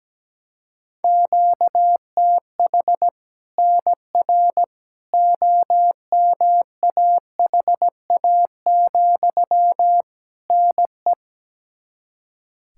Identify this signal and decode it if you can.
Morse code, heard as a keyed tone, about 17 wpm, 700 Hz. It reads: QTH NR OMAHA, NE
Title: Practice QSO - Exchange 1 - Beginning - 17wpm